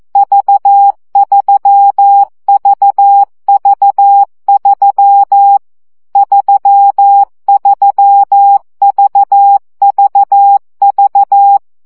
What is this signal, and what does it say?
Morse code, heard as a keyed tone, about 14 wpm, 800 Hz. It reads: V3VV3 33VVV
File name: CW gyakorlas v 3 12 baud